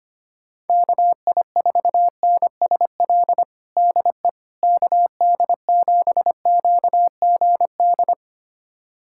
Morse code, heard as a keyed tone, about 25 wpm, 700 Hz. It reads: KI4NSL DE KD7QGD
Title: USA Callsign DE Callsign - 6 Characters - 25wpm